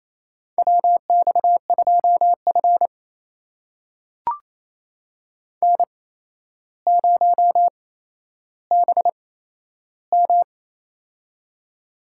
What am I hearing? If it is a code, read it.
Morse code, heard as a keyed tone, about 28 wpm, spaced as for 12 wpm, 700 Hz. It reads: WX2F N0BM
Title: USA Call Signs - 4 Characters - 8x Character Spacing - 28wpm